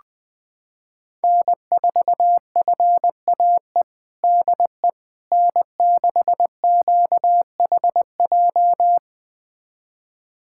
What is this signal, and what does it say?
Morse code, heard as a keyed tone, about 21 wpm, 700 Hz. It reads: N4FAE DE N6QHJ
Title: USA Callsign DE Callsign - 5 Characters - 20wpm